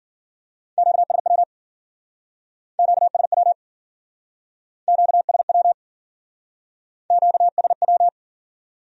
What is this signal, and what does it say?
Morse code, heard as a keyed tone, about 50 wpm, 700 Hz. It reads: QSW QSW QSW QSW